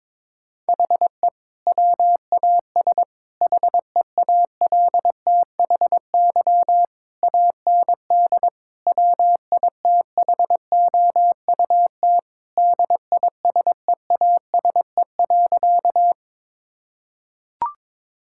Morse code, heard as a keyed tone, about 22 wpm, 700 Hz. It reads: HE WAS HEALTHY AND WITHOUT DISEASE.
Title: Sentences from Top 1000 Words - Review Part 6 - 22wpm